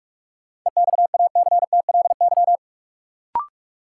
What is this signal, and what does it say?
Morse code, heard as a keyed tone, about 46 wpm, 700 Hz. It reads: EXACTLY
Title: Sentences from Top 900 Words - New Word Drill - 45wpm